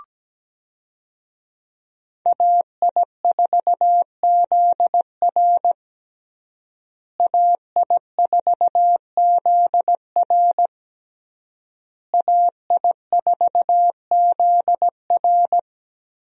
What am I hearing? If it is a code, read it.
Morse code, heard as a keyed tone, about 17 wpm, 700 Hz. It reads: AI4ZR AI4ZR AI4ZR
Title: USA Call Signs - 5 Characters - Repeated 3x - 17wpm